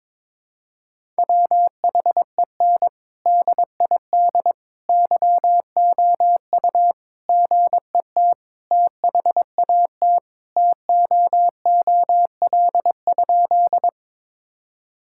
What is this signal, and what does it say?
Morse code, heard as a keyed tone, about 21 wpm, 700 Hz. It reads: WHEN DID YOU GET THAT TOOL?
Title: Sentences from Top 1000 Words - Review Part 6 - 22wpm